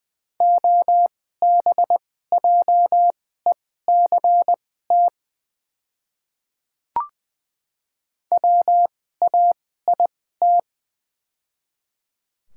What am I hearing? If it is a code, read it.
Morse code, heard as a keyed tone, about 20 wpm, spaced as for 14 wpm, 700 Hz. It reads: OBJECT WAIT
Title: 2x Character Spacing - Top 500 Words - 20wpm